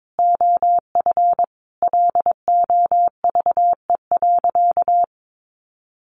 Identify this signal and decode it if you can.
Morse code, heard as a keyed tone, about 22 wpm, 700 Hz. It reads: OF LOVE.
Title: Sentences from Top 1400 Words 22wpm